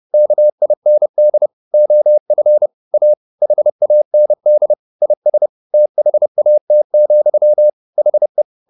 Morse code, heard as a keyed tone, about 30 wpm, 600 Hz. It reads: KIND OF A HAND IS THAT, HE
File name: the old man and the sea / the old man and the sea 1 30wpm